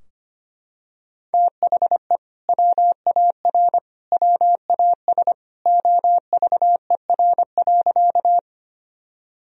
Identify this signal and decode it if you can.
Morse code, heard as a keyed tone, about 25 wpm, 700 Hz. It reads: THE WAR WAS OVER.
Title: Sentences from Top 1000 Words - Review Part 2 - 25wpm